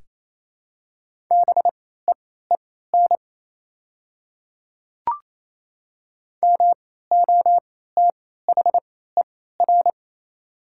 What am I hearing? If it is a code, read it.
Morse code, heard as a keyed tone, about 28 wpm, spaced as for 16 wpm, 690 Hz. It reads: BEEN MOTHER
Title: 3x Character Spacing - Top 200 Words - 28wpm